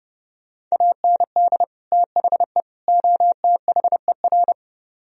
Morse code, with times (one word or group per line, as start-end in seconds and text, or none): 0.7-1.6 AND
1.9-2.6 THE
2.9-4.5 OTHER